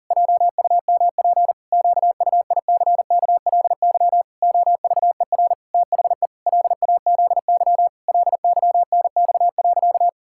Morse code, heard as a keyed tone, about 40 wpm, 700 Hz. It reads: JUMP QUICKLY OVER THE LAZY LYNX.
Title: Pangram Challenge v2 - 40wpm